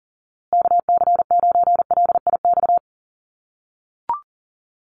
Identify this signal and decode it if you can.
Morse code, heard as a keyed tone, about 39 wpm, 700 Hz. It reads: KC9LIX
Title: USA Call Signs Repeated 3x 40wpm